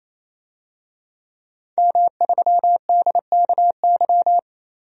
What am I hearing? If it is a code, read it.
Morse code, heard as a keyed tone, about 28 wpm, 700 Hz. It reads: M3DKY
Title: DXCC-100 Call Signs - 5 Characters - 28wpm